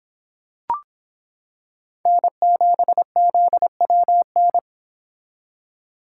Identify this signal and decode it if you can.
Morse code, heard as a keyed tone, about 26 wpm, 700 Hz. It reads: N7ZWN